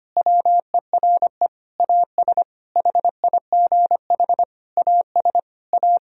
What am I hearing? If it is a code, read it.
Morse code, heard as a keyed tone, about 25 wpm, 700 Hz. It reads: WERE AS HIGH AS A
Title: Sentences from Top 1000 Words - Review Part 1 - 25wpm